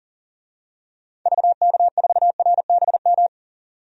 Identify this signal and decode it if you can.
Morse code, heard as a keyed tone, about 40 wpm, 700 Hz. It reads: VK4RBM
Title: DXCC-100 Call Signs - 6 Characters - 40wpm